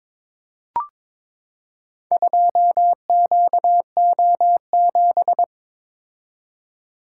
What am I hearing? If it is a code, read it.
Morse code, heard as a keyed tone, about 23 wpm, 700 Hz. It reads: 2QO7
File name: practice / 22wpm / Four Letter-Number Combinations - 22wpm